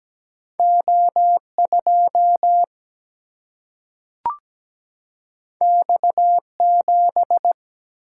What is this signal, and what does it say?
Morse code, heard as a keyed tone, about 17 wpm, 700 Hz. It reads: O2 X7
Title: Single-Letter Single-Number 17wpm